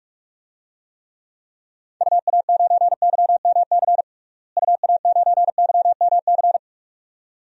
Morse code, heard as a keyed tone, about 46 wpm, 700 Hz. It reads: UA9YMC UA9YMC
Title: DXCC-100 Call Signs - 6 Characters - Repeated 2x - 45wpm